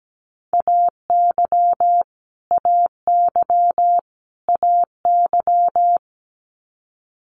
Morse code, heard as a keyed tone, about 17 wpm, 700 Hz. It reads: AY AY AY
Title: Multi-letter Phonemes Repeated 3x 17wpm